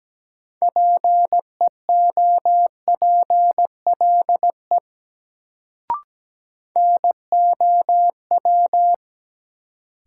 Morse code, heard as a keyed tone, about 17 wpm, 700 Hz. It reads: PEOPLE NOW
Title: Sets of 2 Words - New Word Drill - 17wpm